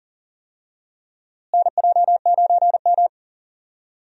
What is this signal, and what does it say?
Morse code, heard as a keyed tone, about 40 wpm, 700 Hz. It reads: NJ9M
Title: USA Call Signs - 4 Characters - Repeated 3x - 40wpm